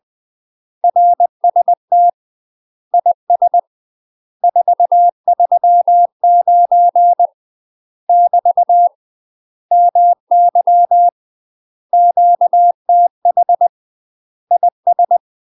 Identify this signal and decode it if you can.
Morse code, heard as a keyed tone, about 20 wpm, 700 Hz. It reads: RST IS 439 <BT> MY QTH IS
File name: UK MASTER QSOs TRAINING / 20WPM / UK MASTER QSOs 0011